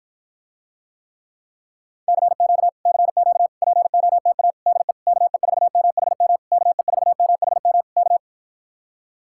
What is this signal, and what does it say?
Morse code, heard as a keyed tone, about 52 wpm, 700 Hz. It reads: CQ CQ POTA DE KE4MHM KE4MHM K